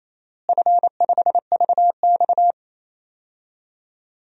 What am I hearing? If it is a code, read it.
Morse code, heard as a keyed tone, about 27 wpm, 700 Hz. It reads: F5VX
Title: DXCC-100 Call Signs - 4 Characters - Repeated 3x - 28wpm